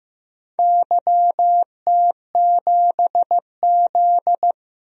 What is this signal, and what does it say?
Morse code, heard as a keyed tone, about 15 wpm, 700 Hz. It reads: YT7Z
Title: DXCC-100 Call Signs - 4 Characters - 15wpm